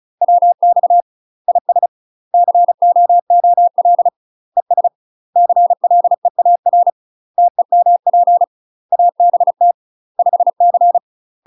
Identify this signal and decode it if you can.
Morse code, heard as a keyed tone, about 36 wpm, 700 Hz. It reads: WX IS COOL ES CLEAR TEMP ABT 5C
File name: UK MASTER QSOs TRAINING / 35WPM / UK MASTER QSOs 0041